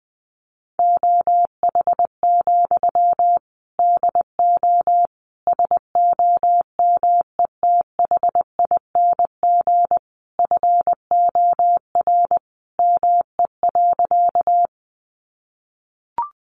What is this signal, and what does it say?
Morse code, heard as a keyed tone, about 20 wpm, 700 Hz. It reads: OH, DO SOMETHING FOR ME.
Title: Sentences from Top 600 Words 20wpm